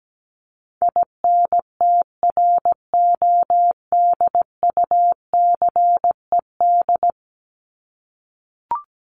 Morse code, heard as a keyed tone, about 17 wpm, 700 Hz. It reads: INTRODUCED
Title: ED Suffix - Top 100 Words with Base Words 17wpm